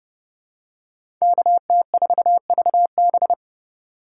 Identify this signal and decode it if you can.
Morse code, heard as a keyed tone, about 31 wpm, 700 Hz. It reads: KT4VB
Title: USA Call Signs - 3x Character Spacing - 30wpm